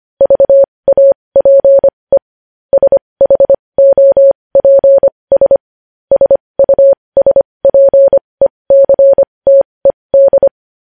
Morse code, heard as a keyed tone, about 25 wpm, spaced as for 20 wpm, 550 Hz. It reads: VAPE SHOPS SUSPECTED
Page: News Headlines in Morse Code at 20 WPM